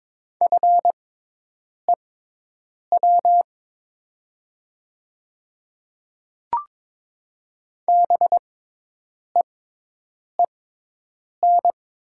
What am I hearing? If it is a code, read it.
Morse code, heard as a keyed tone, about 22 wpm, spaced as for 8 wpm, 700 Hz. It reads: FEW BEEN
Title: 6x Character Spacing - Top 300 Words - 22wpm